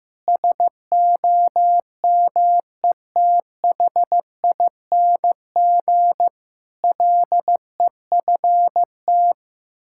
Morse code, heard as a keyed tone, about 16 wpm, 700 Hz. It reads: SOMETHING LEFT
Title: Sentences from Top 1000 Words - Review Part 6 - 15wpm